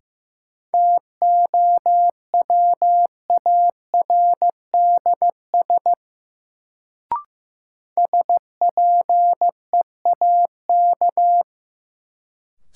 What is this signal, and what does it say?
Morse code, heard as a keyed tone, about 15 wpm, 700 Hz. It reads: TOWARDS SPEAK
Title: Sentences from Top 1000 Words - Review Part 2 - New Word Drill - 15wpm